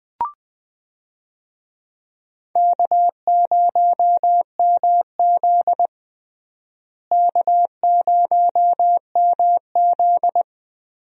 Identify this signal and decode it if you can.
Morse code, heard as a keyed tone, about 20 wpm, 700 Hz. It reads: K0MZ K0MZ
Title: USA Call Signs - 4 Characters - Repeated 3x - 20wpm